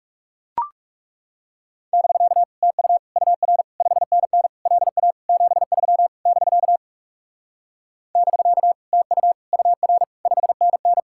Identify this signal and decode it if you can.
Morse code, heard as a keyed tone, about 40 wpm, 700 Hz. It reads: <BK> TU UR 5NN LA 73 <BK> <BK> TU UR 5NN